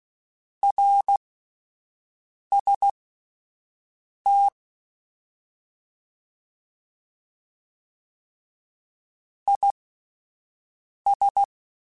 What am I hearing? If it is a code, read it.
Morse code, heard as a keyed tone, about 16 wpm, spaced as for 5 wpm, 800 Hz. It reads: RST IS
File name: (spaced) / QSOs Spaced-mp3s / WA7QOX 1 @10 sp